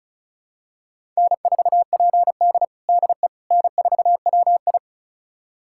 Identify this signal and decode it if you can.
Morse code, heard as a keyed tone, about 36 wpm, 700 Hz. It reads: N4PD DE N4WI
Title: USA Callsign DE Callsign - 4 Characters - 35wpm